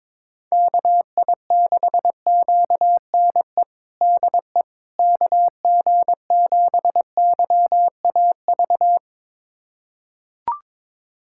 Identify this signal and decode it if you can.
Morse code, heard as a keyed tone, about 22 wpm, 700 Hz. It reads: KI6QNE DE KG7YAV
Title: USA Callsign DE Callsign 22wpm